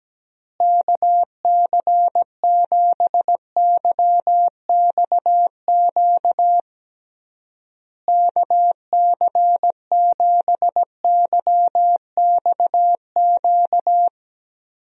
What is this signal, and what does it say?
Morse code, heard as a keyed tone, about 17 wpm, 700 Hz. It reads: KC7YXQ KC7YXQ
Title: USA Call Signs - 6 Characters - Repeated 3x - 17wpm